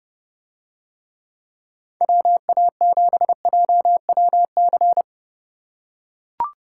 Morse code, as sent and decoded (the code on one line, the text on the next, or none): .-- .- --... .--- .-- -.-.
WA7JWC